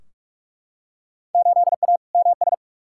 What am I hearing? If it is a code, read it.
Morse code, heard as a keyed tone, about 43 wpm, 700 Hz. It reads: 8A MS
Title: Field Day Exchanges - 45wpm